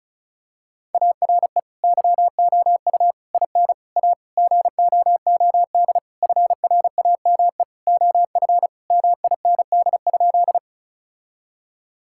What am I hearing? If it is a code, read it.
Morse code, heard as a keyed tone, about 36 wpm, 700 Hz. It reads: ARE YOU IN A GOOD FRAME OF MIND?
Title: Sentences from Top 1000 Words - Review Part 6 - 35wpm